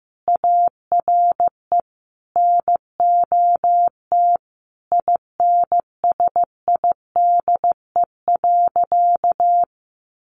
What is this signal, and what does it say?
Morse code, heard as a keyed tone, about 15 wpm, 700 Hz. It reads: ARE NOT INSIDE.
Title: Additional Easy Phrases 15wpm